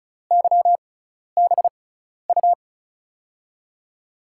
Morse code, heard as a keyed tone, about 36 wpm, spaced as for 12 wpm, 700 Hz. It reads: YBU
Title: USA Call Signs - 5 Characters - 6x Character Spacing - 35wpm